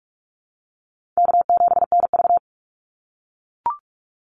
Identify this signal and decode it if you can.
Morse code, heard as a keyed tone, about 44 wpm, 700 Hz. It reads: K7NV
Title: Call Signs - 45wpm